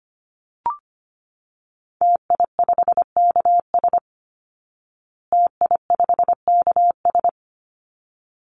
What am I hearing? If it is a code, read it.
Morse code, heard as a keyed tone, about 25 wpm, 700 Hz. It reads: TI5KS TI5KS
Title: Call Signs Repeated 3x - 25wpm